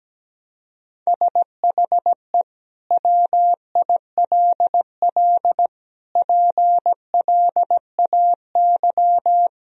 Morse code, heard as a keyed tone, about 17 wpm, 700 Hz. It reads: SHE WILL PLAY
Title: Sentences from Top 1000 Words - Review Part 4 - 17wpm